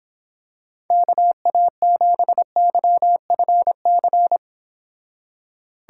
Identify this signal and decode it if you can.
Morse code, heard as a keyed tone, about 26 wpm, 700 Hz. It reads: KA7YFC